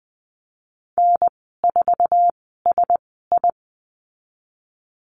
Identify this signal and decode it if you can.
Morse code, heard as a keyed tone, about 20 wpm, spaced as for 15 wpm, 700 Hz. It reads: N4SI
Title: 088 Lesson 15g - Number 5 - Call Signs 20wpm